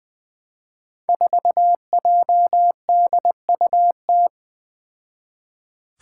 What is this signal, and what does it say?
Morse code, heard as a keyed tone, about 20 wpm, 700 Hz. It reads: 4JDUT
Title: Five Challenging Letter-Number Combinations 20wpm